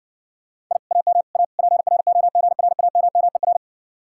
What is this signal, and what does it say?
Morse code, heard as a keyed tone, about 69 wpm, 700 Hz. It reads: I AM A PROGRAMMER